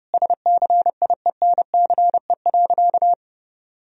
Morse code, heard as a keyed tone, about 32 wpm, 700 Hz. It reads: SCIENCE.
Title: Sentences from Top 1000 Words - Review Part 2 - 30wpm